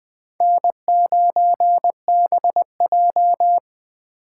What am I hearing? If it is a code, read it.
Morse code, heard as a keyed tone, about 20 wpm, 700 Hz. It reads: N9BJ